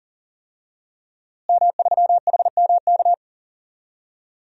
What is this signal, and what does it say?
Morse code, heard as a keyed tone, about 40 wpm, 700 Hz. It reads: M3HMK
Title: DXCC-100 Call Signs - 5 Characters - Repeated 2x - 40wpm